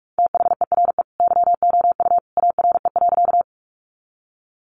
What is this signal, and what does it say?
Morse code, heard as a keyed tone, about 44 wpm, 700 Hz. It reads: THERE YOU ARE.
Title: Sentences from Top 100 Words 45wpm